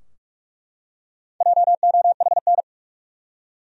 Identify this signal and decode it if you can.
Morse code, heard as a keyed tone, about 43 wpm, 700 Hz. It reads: JOHN
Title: USA Common Names - Max 4 Letters - 45wpm